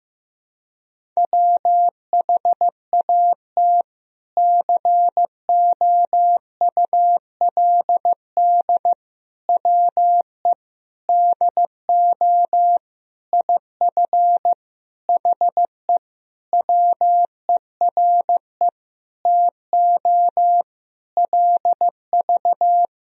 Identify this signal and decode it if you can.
Morse code, heard as a keyed tone, about 16 wpm, 700 Hz. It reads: WHAT COULD WE DO IF HE WERE TO LV
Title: Sentences from Top 400 Words and Abbreviations 15wpm